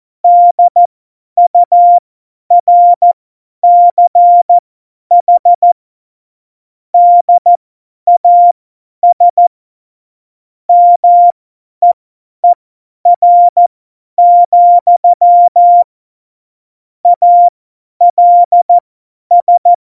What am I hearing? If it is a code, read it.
Morse code, heard as a keyed tone, about 14 wpm, spaced as for 10 wpm, 700 Hz. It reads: DURCH DAS MEER, ALS
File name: Morsekurs / Witze / Witz 03.Juli 2018, 50 F70